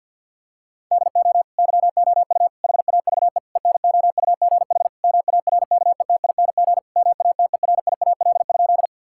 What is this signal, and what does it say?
Morse code, heard as a keyed tone, about 50 wpm, 700 Hz. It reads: DO YOU HAVE ENOUGH MARKETING MATERIAL?
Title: Sentences from Top 1000 Words - Review Part 5 - 50wpm